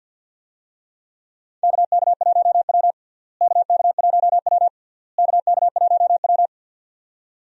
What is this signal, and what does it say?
Morse code, heard as a keyed tone, about 46 wpm, 700 Hz. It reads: KK1W KK1W KK1W
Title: USA Call Signs - 4 Characters - Repeated 3x - 50wpm